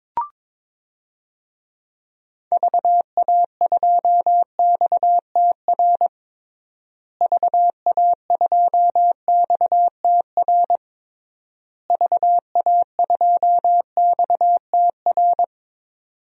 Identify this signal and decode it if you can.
Morse code, heard as a keyed tone, about 21 wpm, 700 Hz. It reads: VA2XTR VA2XTR VA2XTR
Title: DXCC-100 Call Signs - 6 Characters - Repeated 3x - 22wpm